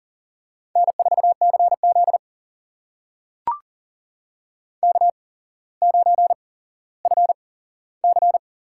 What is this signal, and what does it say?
Morse code, heard as a keyed tone, about 43 wpm, spaced as for 17 wpm, 700 Hz. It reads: N4CZ K9FC
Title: USA Call Signs - 4 Characters - 8x Character Spacing - 40wpm